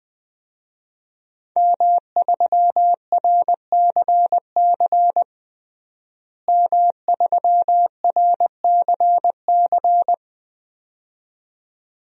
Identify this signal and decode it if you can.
Morse code, heard as a keyed tone, about 20 wpm, 700 Hz. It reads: M3RCC M3RCC
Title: DXCC-100 Call Signs - 5 Characters - Repeated 2x - 20wpm